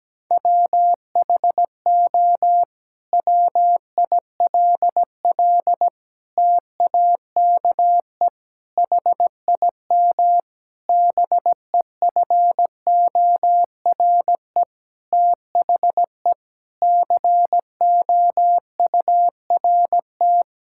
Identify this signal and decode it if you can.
Morse code, heard as a keyed tone, about 17 wpm, 700 Hz. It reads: WHO WILL TAKE HIM BEFORE THE COURT
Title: Sentences from Top 1600 Words 17wpm